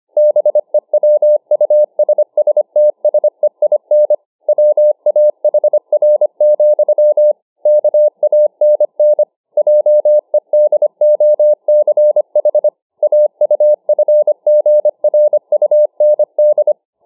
Morse code, heard as a keyed tone, about 25 wpm, 600 Hz. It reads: BEWUSSTSEIN WAHR, KANN JEDOCH AUFGRUND